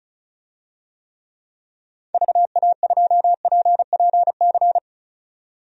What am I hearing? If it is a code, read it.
Morse code, heard as a keyed tone, about 36 wpm, 700 Hz. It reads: VA2PPC
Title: DXCC-100 Call Signs - 6 Characters - 35wpm